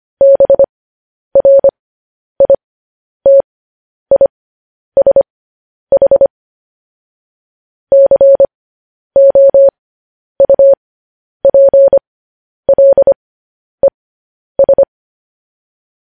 Morse code, heard as a keyed tone, about 25 wpm, spaced as for 10 wpm, 550 Hz. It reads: BRITISH COUPLES
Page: news headlines in morse code at 10 wpm